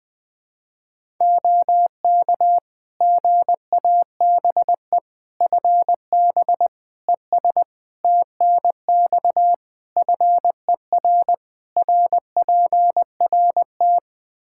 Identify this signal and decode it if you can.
Morse code, heard as a keyed tone, about 20 wpm, 700 Hz. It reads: OK GABE FB ES TNX FER RPRT
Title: Practice QSO - Exchange 2 - Info - 20wpm